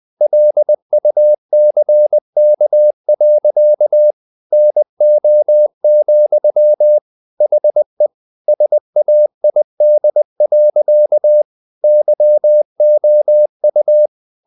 Morse code, heard as a keyed tone, about 20 wpm, 600 Hz. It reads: LUCK. NO, HE SAID. YOU
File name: the old man and the sea part 2 20wpm